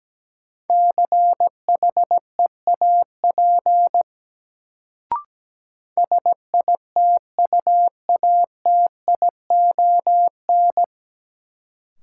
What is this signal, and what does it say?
Morse code, heard as a keyed tone, about 17 wpm, 700 Hz. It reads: CHEAP SITUATION
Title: Sentences from Top 1200 Words - New Word Drill - 17wpm